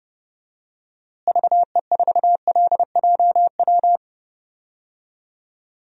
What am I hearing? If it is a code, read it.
Morse code, heard as a keyed tone, about 30 wpm, 700 Hz. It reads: VE4LJW
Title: DXCC-100 Call Signs - 6 Characters - Repeated 2x - 30wpm